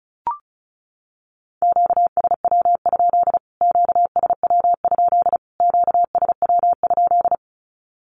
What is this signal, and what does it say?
Morse code, heard as a keyed tone, about 34 wpm, 700 Hz. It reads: QSW? QSW? QSW?
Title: Q-Codes Repeated 3x 35wpm